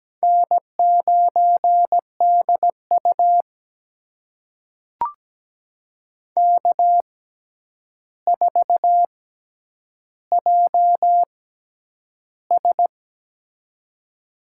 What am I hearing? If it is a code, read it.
Morse code, heard as a keyed tone, about 17 wpm, spaced as for 9 wpm, 700 Hz. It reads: N9DU K4JS
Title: USA Call Signs - 4 Characters - 6x Character Spacing - 17wpm